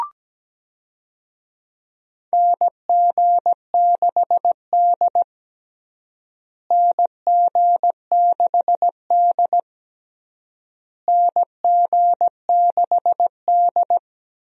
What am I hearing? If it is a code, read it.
Morse code, heard as a keyed tone, about 17 wpm, 700 Hz. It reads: NG6D NG6D NG6D
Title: USA Call Signs - 4 Characters - Repeated 3x - 17wpm